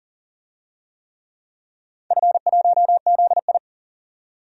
Morse code, heard as a keyed tone, about 38 wpm, 700 Hz. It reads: F1ZI
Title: DXCC-100 Call Signs - 4 Characters - 40wpm